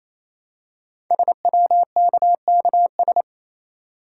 Morse code, heard as a keyed tone, about 27 wpm, 700 Hz. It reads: SWKKS
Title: Code Groups - 5 Letters - 28wpm